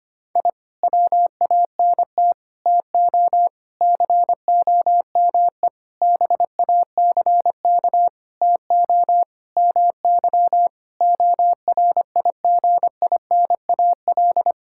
Morse code, heard as a keyed tone, about 25 wpm, 700 Hz. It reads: I WANT TO COME BACK TO MY ORIGINAL
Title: Sentences from Top 1000 Words - Review Part 6 - 25wpm